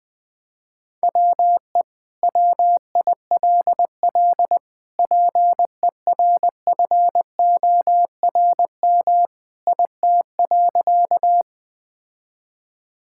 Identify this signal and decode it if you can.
Morse code, heard as a keyed tone, about 21 wpm, 700 Hz. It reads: WE WILL PERFORM IT.
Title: Sentences from Top 1000 Words - Review Part 5 - 20wpm